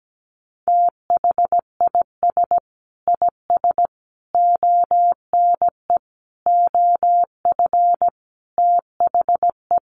Morse code, heard as a keyed tone, about 17 wpm, 700 Hz. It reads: THIS IS ONE OF THE
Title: Sets of 5 Words 17wpm